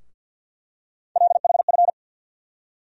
Morse code, heard as a keyed tone, about 52 wpm, 700 Hz. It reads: LHF
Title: USA Call Sign Postfixes - 3 Letters 50wpm